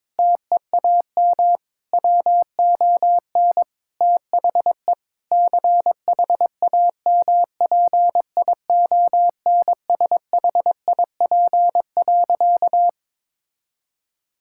Morse code, heard as a keyed tone, about 22 wpm, 700 Hz. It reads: TEAM WON THE CHAMPIONSHIP.
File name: practice / 22wpm / Sentences with 8-Word Phrases - 22wpm